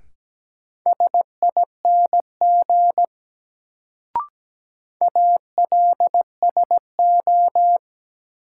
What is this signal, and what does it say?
Morse code, heard as a keyed tone, about 17 wpm, 700 Hz. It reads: SING ALSO
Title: Sentences from Top 400 Words - New Word Drill - 17wpm